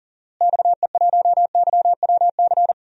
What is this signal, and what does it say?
Morse code, heard as a keyed tone, about 39 wpm, 700 Hz. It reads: XE1YWC
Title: DXCC-100 Call Signs - 6 Characters - Repeated 3x - 40wpm